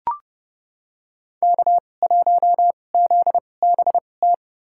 Morse code, heard as a keyed tone, about 30 wpm, spaced as for 22 wpm, 700 Hz. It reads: K1ZBT
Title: USA Call Signs - 2x Character Spacing - 30wpm